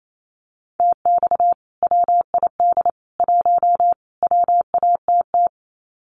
Morse code, heard as a keyed tone, about 28 wpm, 700 Hz. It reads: TX WID 1 WATT
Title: Common QSO Phrases 28wpm